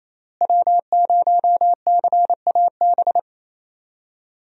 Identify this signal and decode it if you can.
Morse code, heard as a keyed tone, about 28 wpm, 700 Hz. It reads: W0CAB
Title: USA Call Signs - 5 Characters - Repeated 3x - 28wpm